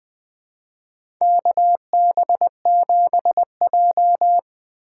Morse code, heard as a keyed tone, about 20 wpm, 700 Hz. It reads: KB7J
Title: USA Call Signs - 4 Characters - 4x Character Spacing - 20wpm